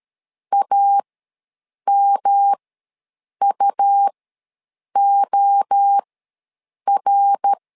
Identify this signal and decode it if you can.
Morse code, heard as a keyed tone, about 13 wpm, spaced as for 7 wpm, 800 Hz. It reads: AMUOR
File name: LEZIONI / 3 MWRO / 3 MWRO ws 12wpm 1914